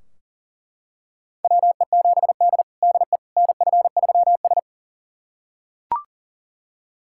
Morse code, heard as a keyed tone, about 40 wpm, 700 Hz. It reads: WE7D DE NF3S
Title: USA Callsign DE Callsign - 4 Characters - 40wpm